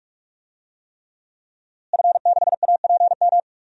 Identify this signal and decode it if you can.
Morse code, heard as a keyed tone, about 46 wpm, 700 Hz. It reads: F6APM
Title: DXCC-100 Call Signs - 5 Characters - Repeated 3x - 45wpm